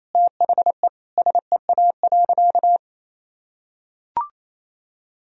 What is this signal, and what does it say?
Morse code, heard as a keyed tone, about 28 wpm, 700 Hz. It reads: THE SEA.
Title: Sentences from Top 1000 Words - Review Part 2 - 28wpm